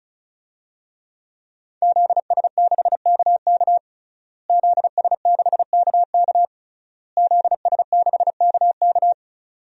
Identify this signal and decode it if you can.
Morse code, heard as a keyed tone, about 36 wpm, 700 Hz. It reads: ZS6KK ZS6KK ZS6KK
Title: DXCC-100 Call Signs - 5 Characters - Repeated 3x - 35wpm